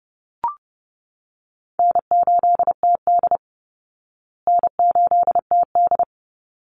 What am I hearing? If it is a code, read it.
Morse code, heard as a keyed tone, about 30 wpm, 700 Hz. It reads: N8TD N8TD
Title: Easy Call Signs Repeated 2x - 30wpm